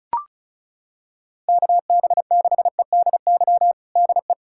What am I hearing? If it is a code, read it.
Morse code, heard as a keyed tone, about 36 wpm, 700 Hz. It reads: KB6EDY DE
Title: USA Callsign DE Callsign - 6 Characters - 35wpm